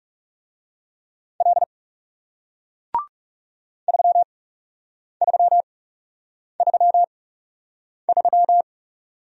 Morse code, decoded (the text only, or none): L 3333